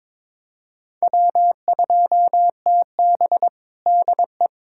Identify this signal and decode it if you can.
Morse code, heard as a keyed tone, about 22 wpm, 700 Hz. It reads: W2TB DE
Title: USA Callsign DE Callsign - 4 Characters - 22wpm